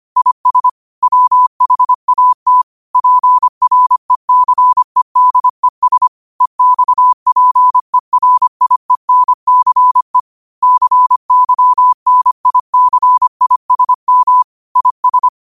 Morse code, heard as a keyed tone, about 25 wpm, 1000 Hz. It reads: IS WHAT PRECEDES EXPERIENCE CYNICISM IS
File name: qotdfiles / 25wpm